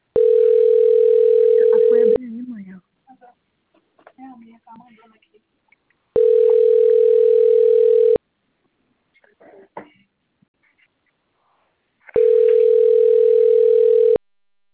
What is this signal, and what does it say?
Ringback tone (North America), 440 + 480 Hz, with speech.
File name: out- - - - - .91082